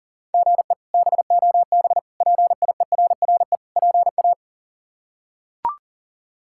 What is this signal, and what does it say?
Morse code, heard as a keyed tone, about 40 wpm, 700 Hz. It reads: GE BOB PIERRE PA
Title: K1USN SST QSO Practice - 40wpm